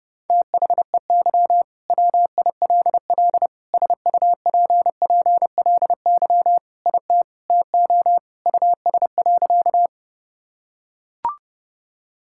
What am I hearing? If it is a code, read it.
Morse code, heard as a keyed tone, about 32 wpm, 700 Hz. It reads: THEY WILL SUPPLY IT TO US.
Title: Sentences from Top 1000 Words - Review Part 5 - 30wpm